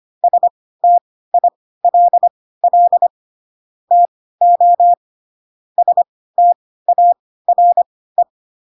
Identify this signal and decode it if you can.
Morse code, heard as a keyed tone, about 25 wpm, spaced as for 16 wpm, 700 Hz. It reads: STILL TO STARE
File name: Pinocchio / Pinocchio ch3 10 25 ik1hge